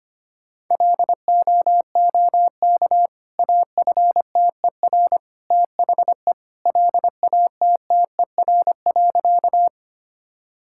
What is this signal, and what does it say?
Morse code, heard as a keyed tone, about 25 wpm, 700 Hz. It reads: LOOK AFTER THE LATTER.
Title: Sentences from Top 1600 Words 25wpm